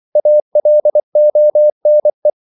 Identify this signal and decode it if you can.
Morse code, heard as a keyed tone, about 24 wpm, 600 Hz. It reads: ALONE
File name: CW / Practice / 500MostCommonEnglishWords24